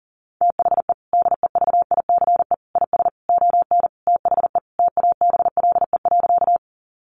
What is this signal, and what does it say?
Morse code, heard as a keyed tone, about 40 wpm, 700 Hz. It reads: THE DEVICE IS ON THE TABLE.
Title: Sentences from Top 1500 Words 40wpm